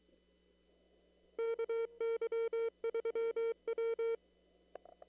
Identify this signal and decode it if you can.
Morse code, heard as a keyed tone, about 23 wpm, 470 Hz. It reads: KY3W
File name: CW / kt5j wpx cw-op-k5pi